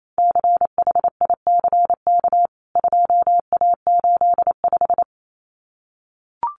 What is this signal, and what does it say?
Morse code, heard as a keyed tone, about 28 wpm, 700 Hz. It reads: CHICK 2A85
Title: CWT Exchange 28wpm